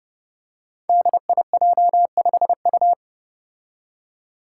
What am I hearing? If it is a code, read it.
Morse code, heard as a keyed tone, about 29 wpm, 700 Hz. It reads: DIJ5U
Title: Five Challenging Letter-Number Combinations 30wpm